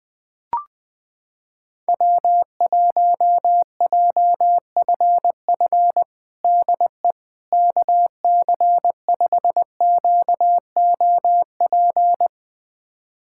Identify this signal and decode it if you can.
Morse code, heard as a keyed tone, about 20 wpm, 700 Hz. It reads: W1JFF DE KC5QOP
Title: USA Callsign DE Callsign 20wpm